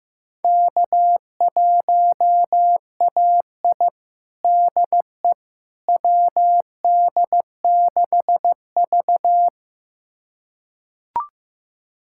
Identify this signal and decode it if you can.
Morse code, heard as a keyed tone, about 15 wpm, 700 Hz. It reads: K1AI DE WD6V
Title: USA Callsign DE Callsign - 4 Characters - 15wpm